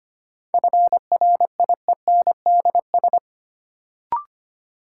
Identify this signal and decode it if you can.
Morse code, heard as a keyed tone, about 25 wpm, 700 Hz. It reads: FRIENDS
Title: Sentences from Top 1600 Words - New Word Drill Alt - 25wpm